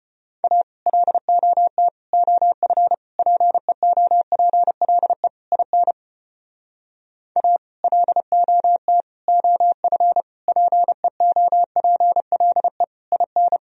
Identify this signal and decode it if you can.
Morse code, heard as a keyed tone, about 33 wpm, 700 Hz. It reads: A LOT OF PEOPLE IN A LOT OF PEOPLE IN